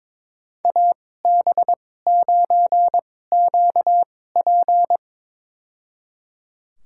Morse code, heard as a keyed tone, about 22 wpm, spaced as for 16 wpm, 700 Hz. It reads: AB9QP
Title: USA Call Signs - 5 Characters - 2x Character Spacing - 22wpm